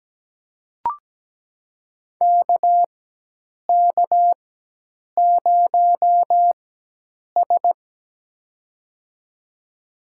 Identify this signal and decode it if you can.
Morse code, heard as a keyed tone, about 17 wpm, spaced as for 8 wpm, 700 Hz. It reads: KK0S
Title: USA Call Signs - 4 Characters - 4x Character Spacing - 17wpm